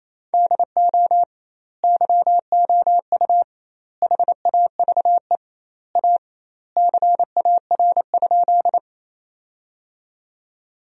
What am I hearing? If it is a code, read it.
Morse code, heard as a keyed tone, about 28 wpm, 700 Hz. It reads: DO YOU HAVE A CAR?
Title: 2x Word-Spacing - Sentences from Top 300 Words - 28wpm